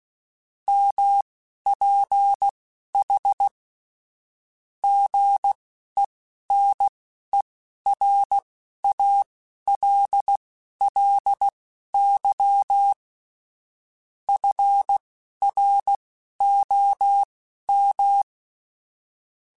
Morse code, heard as a keyed tone, about 16 wpm, spaced as for 11 wpm, 800 Hz. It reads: MPH GENERALLY FROM